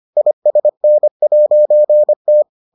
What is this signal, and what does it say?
Morse code, heard as a keyed tone, about 25 wpm, 600 Hz. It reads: ISN'T